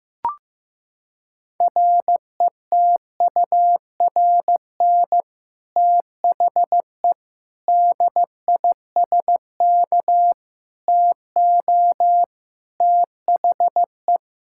Morse code, heard as a keyed tone, about 15 wpm, 700 Hz. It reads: RETURN THE DISK TO THE
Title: Sentences from Top 1000 Words - Review Part 6 - 15wpm